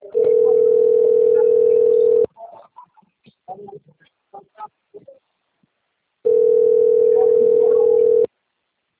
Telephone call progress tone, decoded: ringback tone

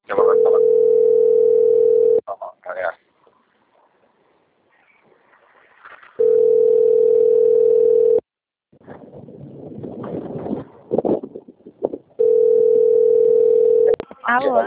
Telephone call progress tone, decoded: ringback tone

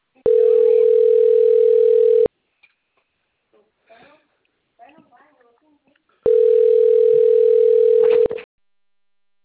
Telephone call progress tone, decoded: ringback tone

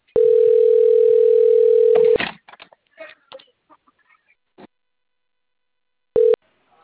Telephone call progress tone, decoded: ringback tone

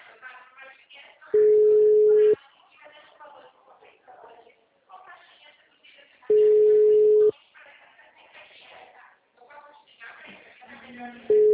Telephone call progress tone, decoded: ringback tone

